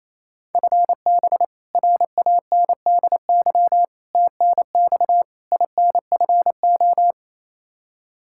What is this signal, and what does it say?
Morse code, heard as a keyed tone, about 28 wpm, 700 Hz. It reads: FB RANDY TNX INFO
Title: K1USN SST QSO Practice 28wpm